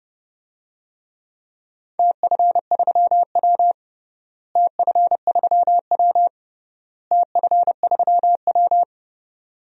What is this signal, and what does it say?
Morse code, heard as a keyed tone, about 32 wpm, 700 Hz. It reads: TF3W TF3W TF3W
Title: DXCC-100 Call Signs - 4 Characters - Repeated 3x - 30wpm